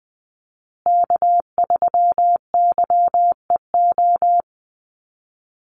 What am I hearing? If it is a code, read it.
Morse code, heard as a keyed tone, about 19 wpm, 700 Hz. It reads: K3YEO
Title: Call Signs Repeated 2x - 20wpm